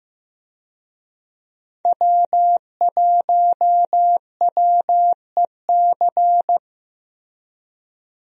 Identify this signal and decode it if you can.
Morse code, heard as a keyed tone, about 15 wpm, 700 Hz. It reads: W1WEC
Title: USA Call Signs - 5 Characters - Repeated 3x - 15wpm